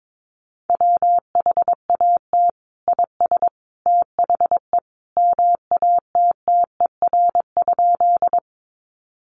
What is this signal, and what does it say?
Morse code, heard as a keyed tone, about 22 wpm, 700 Hz. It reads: WHAT IS THE MATTER?
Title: Sentences from Top 700 Words 22wpm